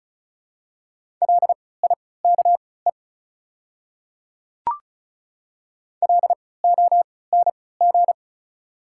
Morse code, heard as a keyed tone, about 36 wpm, spaced as for 20 wpm, 690 Hz. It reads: LIKE LONG
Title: 3x Character Spacing - Top 100 Words - 35wpm